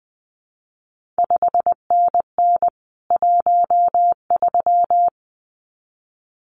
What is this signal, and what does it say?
Morse code, heard as a keyed tone, about 20 wpm, 700 Hz. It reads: 5NN 13
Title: CQ WW DX Contest Exchange Encore - 20wpm